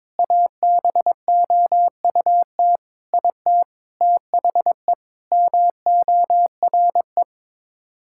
Morse code, heard as a keyed tone, about 22 wpm, 700 Hz. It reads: ABOUT IT THE MORE
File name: practice / 22wpm / Sets of 8 Words - 22wpm